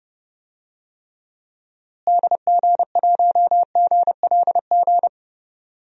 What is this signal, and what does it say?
Morse code, heard as a keyed tone, about 30 wpm, 700 Hz. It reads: DG1GLG
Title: DXCC-100 Call Signs - 6 Characters - Repeated 2x - 30wpm